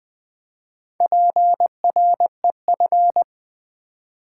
Morse code, heard as a keyed tone, about 20 wpm, 700 Hz. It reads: PREF
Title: Build A Word - 2 Letters At A Time - 6-Letter Words - 20wpm